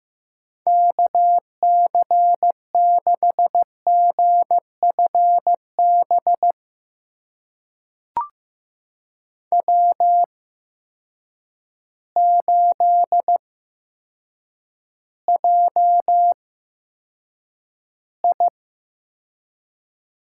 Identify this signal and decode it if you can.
Morse code, heard as a keyed tone, about 15 wpm, spaced as for 8 wpm, 700 Hz. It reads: KC6GFB W8JI
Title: USA Call Signs - 8x Character Spacing - 15wpm